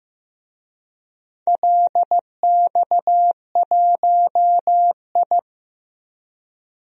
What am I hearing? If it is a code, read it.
Morse code, heard as a keyed tone, about 15 wpm, 700 Hz. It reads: LX1I
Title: DXCC-100 Call Signs - 4 Characters - Repeated 2x - 15wpm